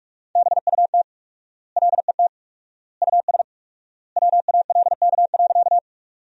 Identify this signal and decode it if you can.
Morse code, heard as a keyed tone, about 41 wpm, 700 Hz. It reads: BUT LET US WALK.
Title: 4x Word-Spacing - Sentences from Top 300 Words - 45wpm